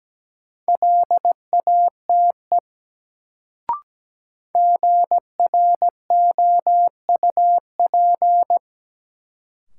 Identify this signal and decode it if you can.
Morse code, heard as a keyed tone, about 17 wpm, 700 Hz. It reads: LATE GROUP
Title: Sets of 3 Words - New Word Drill - 17wpm